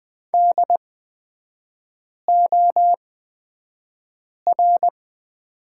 Morse code, heard as a keyed tone, about 20 wpm, spaced as for 5 wpm, 690 Hz. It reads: DOR
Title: 229 Lesson 37e - Letter x - Three Characters 20wpm